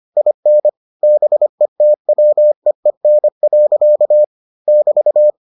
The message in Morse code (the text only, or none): IN BETWEEN. <BT>